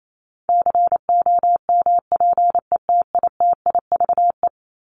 Morse code, heard as a keyed tone, about 28 wpm, 700 Hz. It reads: COMPETITIVE
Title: Top 1001-1250 Words 28wpm